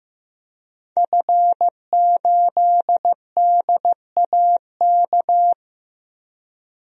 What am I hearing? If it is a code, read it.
Morse code, heard as a keyed tone, about 16 wpm, 700 Hz. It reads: F8DAK
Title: DXCC-100 Call Signs - 5 Characters - Repeated 3x - 15wpm